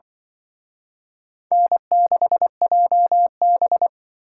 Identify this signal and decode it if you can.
Morse code, heard as a keyed tone, about 24 wpm, 700 Hz. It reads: N6JB
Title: Speed Racing - USA 1x2 Call Signs 20wpm